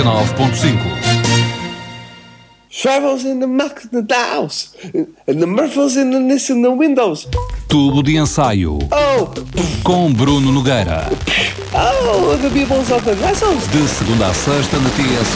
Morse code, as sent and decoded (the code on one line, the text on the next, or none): none